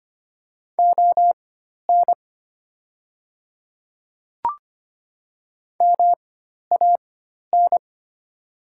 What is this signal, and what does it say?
Morse code, heard as a keyed tone, about 25 wpm, spaced as for 12 wpm, 700 Hz. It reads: ON MAN
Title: 4x Character Spacing - Top 200 Words - 25wpm